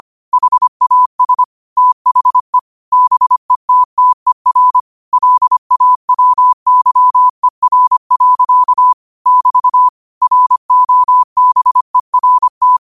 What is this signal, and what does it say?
Morse code, heard as a keyed tone, about 25 wpm, 1000 Hz. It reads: HAS THE BETTER LAWYER. <BT> ROBERT
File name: qotdfiles / 25wpm